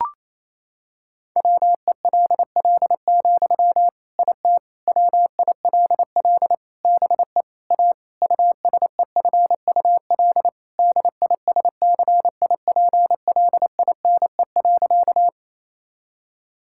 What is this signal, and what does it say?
Morse code, heard as a keyed tone, about 28 wpm, 700 Hz. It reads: WELL, IT WILL BE A USEFUL DISCIPLINE.
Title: Sentences from Top 1000 Words - Review Part 6 - 28wpm